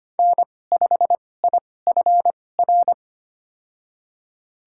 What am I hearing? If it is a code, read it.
Morse code, heard as a keyed tone, about 25 wpm, spaced as for 18 wpm, 700 Hz. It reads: N5IFR
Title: USA Call Signs - 5 Characters - 2x Character Spacing - 25wpm